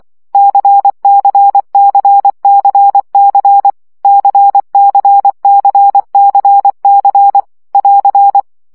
Morse code, heard as a keyed tone, about 24 wpm, 800 Hz. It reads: CCCCC CCCCC <AR>
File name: CW gyakorlas c 20 baud